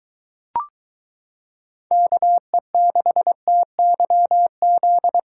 Morse code, heard as a keyed tone, about 23 wpm, 700 Hz. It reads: KE6TYZ